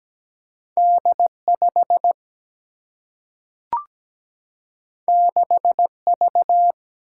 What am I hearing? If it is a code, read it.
Morse code, heard as a keyed tone, about 17 wpm, 700 Hz. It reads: D 5 6 V
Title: Single-Letter Single-Number 17wpm